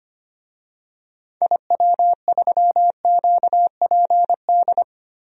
Morse code, heard as a keyed tone, about 25 wpm, 700 Hz. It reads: IW3QPD
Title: DXCC-100 Call Signs - 6 Characters - Repeated 3x - 25wpm